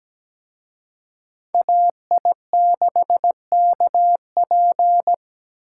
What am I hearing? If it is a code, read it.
Morse code, heard as a keyed tone, about 20 wpm, 700 Hz. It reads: AI6KP
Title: Speed Racing - USA 2x2 Call Signs 17wpm